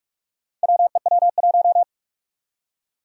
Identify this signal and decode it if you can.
Morse code, heard as a keyed tone, about 46 wpm, 700 Hz. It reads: WEW1
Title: Four Challenging Letter-Number Combinations 45wpm